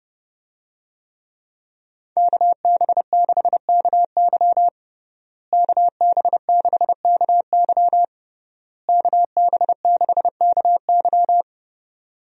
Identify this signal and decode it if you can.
Morse code, heard as a keyed tone, about 30 wpm, 700 Hz. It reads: KB6KY KB6KY KB6KY
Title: USA Call Signs - 5 Characters - Repeated 3x - 30wpm